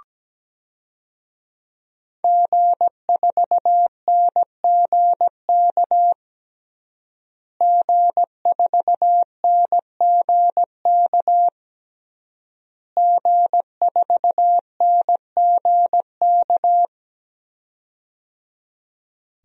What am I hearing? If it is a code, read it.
Morse code, heard as a keyed tone, about 17 wpm, 700 Hz. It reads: G4NGK G4NGK G4NGK
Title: DXCC-100 Call Signs - 5 Characters - Repeated 3x - 17wpm